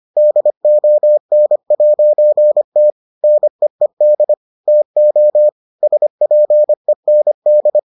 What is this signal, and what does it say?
Morse code, heard as a keyed tone, about 25 wpm, 600 Hz. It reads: DON'T NEED TO SPEND